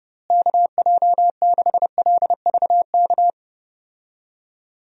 Morse code, heard as a keyed tone, about 30 wpm, 700 Hz. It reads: KJ6LVK
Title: USA Call Signs - 6 Characters - Repeated 3x - 30wpm